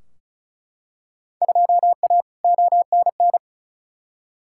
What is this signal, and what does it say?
Morse code, heard as a keyed tone, about 36 wpm, 700 Hz. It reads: 2A ONN
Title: Field Day Exchanges Encore - 35wpm